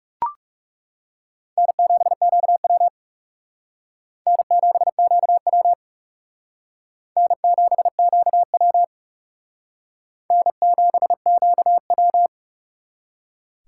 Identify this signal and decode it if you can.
Morse code, heard as a keyed tone, about 39 wpm, 700 Hz. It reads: N7QW N7QW N7QW N7QW